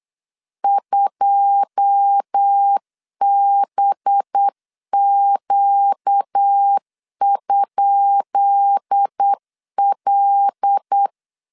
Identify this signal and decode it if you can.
Morse code, heard as a keyed tone, about 8 wpm, 800 Hz. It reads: 2BQ?L
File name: LEZIONI / 10 SIGNS / 10B INT 8wpm 3534